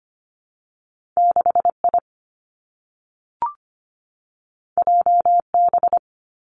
Morse code, heard as a keyed tone, about 25 wpm, 700 Hz. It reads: 6 I J B
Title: Two Challenging Letter-Number Combinations 25wpm